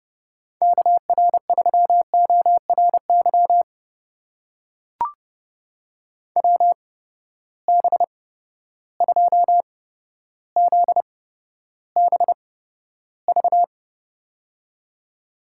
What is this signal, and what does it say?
Morse code, heard as a keyed tone, about 30 wpm, spaced as for 13 wpm, 700 Hz. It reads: KR3ORY WB2ZBV
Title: USA Call Signs - 6 Characters - 8x Character Spacing - 30wpm